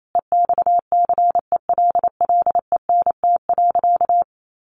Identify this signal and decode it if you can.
Morse code, heard as a keyed tone, about 28 wpm, 700 Hz. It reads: EXCELLENT.
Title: Sentences from Top 1000 Words 28wpm